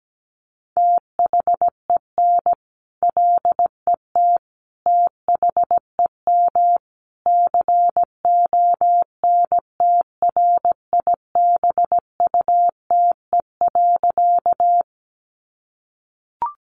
Morse code, heard as a keyed tone, about 17 wpm, 700 Hz. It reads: THEN LET THEM CONTRIBUTE.
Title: Sentences from Top 1400 Words 17wpm